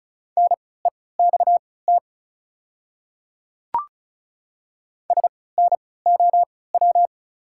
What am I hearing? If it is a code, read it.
Morse code, heard as a keyed tone, about 34 wpm, spaced as for 20 wpm, 690 Hz. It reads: NEXT SNOW
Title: 3x Character Spacing - Top 500 Words - 35wpm